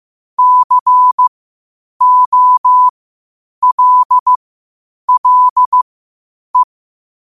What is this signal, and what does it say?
Morse code, heard as a keyed tone, about 15 wpm, spaced as for 8 wpm, 1000 Hz. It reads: COLLE
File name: qotdfiles / 8wpm